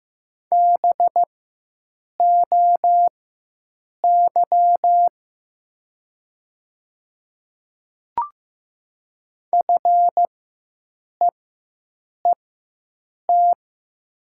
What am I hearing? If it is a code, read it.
Morse code, heard as a keyed tone, about 15 wpm, spaced as for 7 wpm, 700 Hz. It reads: BOY FEET
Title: 4x Character Spacing - Top 300 Words - 15wpm